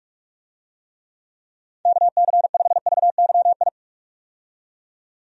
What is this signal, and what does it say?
Morse code, heard as a keyed tone, about 46 wpm, 700 Hz. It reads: KC5VYI
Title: USA Call Signs - 6 Characters - 45wpm